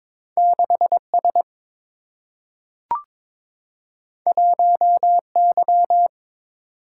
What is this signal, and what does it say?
Morse code, heard as a keyed tone, about 22 wpm, 700 Hz. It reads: 6S 1Y
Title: Single-Letter Single-Number 22wpm